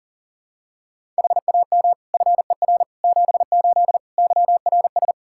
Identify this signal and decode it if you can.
Morse code, heard as a keyed tone, about 40 wpm, 700 Hz. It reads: HAM FER 78 YRS
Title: Practice QSO - Exchange 3 - More Info - 40wpm